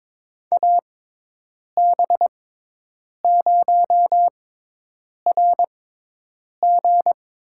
Morse code, heard as a keyed tone, about 22 wpm, spaced as for 8 wpm, 700 Hz. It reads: AB0RG
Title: USA Call Signs - 5 Characters - 6x Character Spacing - 22wpm